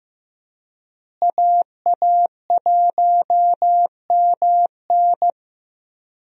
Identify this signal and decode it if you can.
Morse code, heard as a keyed tone, about 14 wpm, 700 Hz. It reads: AA1MN
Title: USA Call Signs - 5 Characters - Repeated 3x - 15wpm